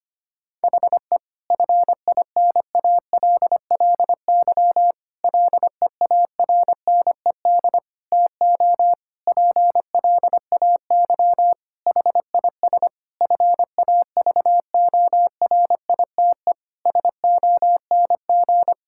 Morse code, heard as a keyed tone, about 25 wpm, 700 Hz. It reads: HE FINALLY LEARNED TO PLAY HIS FAVORITE SONG
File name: practice / 25wpm / Difficult Sentences with Hints - 25wpm